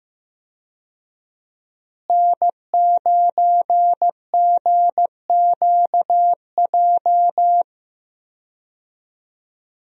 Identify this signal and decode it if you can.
Morse code, heard as a keyed tone, about 15 wpm, 700 Hz. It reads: N9GQJ
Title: USA Call Signs - 5 Characters - Repeated 3x - 15wpm